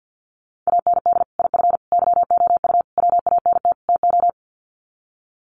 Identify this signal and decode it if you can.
Morse code, heard as a keyed tone, about 50 wpm, 700 Hz. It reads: AND IF YOU WANT TO
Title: Sets of 5 Words Encore 50wpm